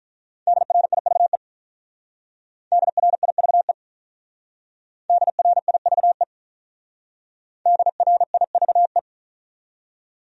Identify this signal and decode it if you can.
Morse code, heard as a keyed tone, about 44 wpm, 700 Hz. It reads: DRIVE DRIVE DRIVE DRIVE